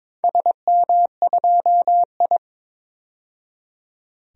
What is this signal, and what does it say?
Morse code, heard as a keyed tone, about 22 wpm, 700 Hz. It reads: SM2I
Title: DXCC-100 Call Signs - 4 Characters - 22wpm